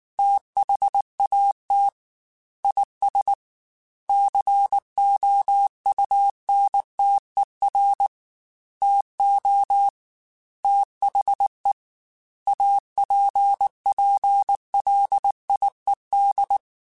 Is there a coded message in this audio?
Yes: Morse code from 0.2 to 16.6 s, heard as a keyed tone, about 19 wpm, 800 Hz. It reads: THAT IS COUNTER TO THE APPLIED